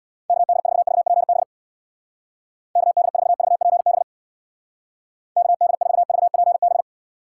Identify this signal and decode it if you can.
Morse code, heard as a keyed tone, about 62 wpm, 700 Hz. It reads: KD4VPB KD4VPB KD4VPB
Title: Speed Racing - USA 2x3 Call Signs 50wpm